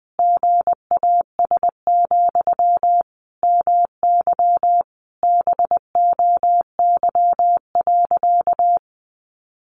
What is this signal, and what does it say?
Morse code, heard as a keyed tone, about 20 wpm, 700 Hz. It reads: GAS, MY BOY.
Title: Sentences from Top 1100 Words 20wpm